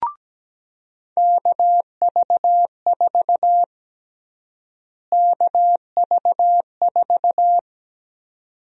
Morse code, heard as a keyed tone, about 17 wpm, 700 Hz. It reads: KV4 KV4
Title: USA Call Sign Prefixes - Repeated 2x - 17wpm